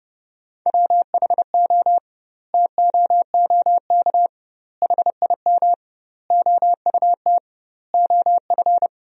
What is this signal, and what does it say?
Morse code, heard as a keyed tone, about 30 wpm, 700 Hz. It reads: WHO TOOK HIM OUT OF
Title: 2x Word-Spacing - Sentences from Top 300 Words - 30wpm